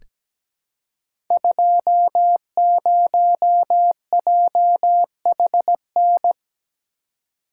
Morse code, heard as a keyed tone, about 17 wpm, 700 Hz. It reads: 20JHN
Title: Code Groups - 5 Letter:Numbers - 17wpm